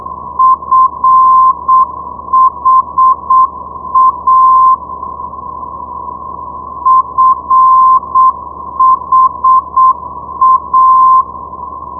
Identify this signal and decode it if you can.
Morse code, heard as a keyed tone, about 7 wpm, 1040 Hz. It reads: FHA FHA
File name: ndb friedrichshafen fha 473.0